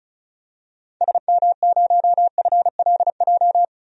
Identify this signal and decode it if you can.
Morse code, heard as a keyed tone, about 34 wpm, 700 Hz. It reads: SM0FLJ
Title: DXCC-100 Call Signs - 6 Characters - 35wpm